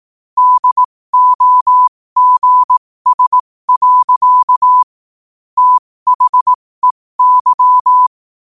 Morse code, heard as a keyed tone, about 18 wpm, spaced as for 15 wpm, 1000 Hz. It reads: DOGS. THEY